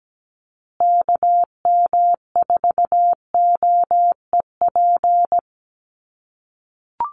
Morse code, heard as a keyed tone, about 17 wpm, 700 Hz. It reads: KM4OEP
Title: USA Call Signs Repeated 3x 17wpm